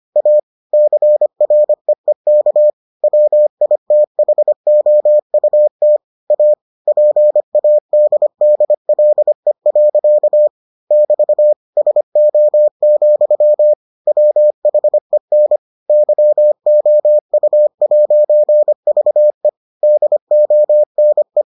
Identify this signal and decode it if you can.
Morse code, heard as a keyed tone, about 25 wpm, 600 Hz. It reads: A CREEK WITHOUT A PADDLE. <BT> SO, WHEN YOU'VE DONE